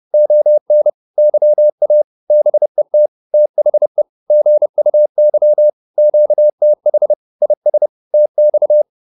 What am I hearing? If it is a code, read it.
Morse code, heard as a keyed tone, about 30 wpm, 600 Hz. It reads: ON YA BET THE GUY QTH IS TX